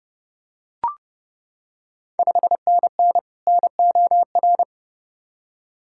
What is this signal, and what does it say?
Morse code, heard as a keyed tone, about 30 wpm, 700 Hz. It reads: 5NN NOR
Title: Ontario QSO Party Exchange 30wpm